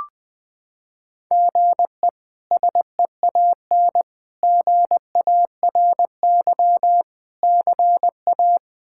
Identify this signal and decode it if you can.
Morse code, heard as a keyed tone, about 21 wpm, 700 Hz. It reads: GE SEAN GARY CA
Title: K1USN SST QSO Practice - 20wpm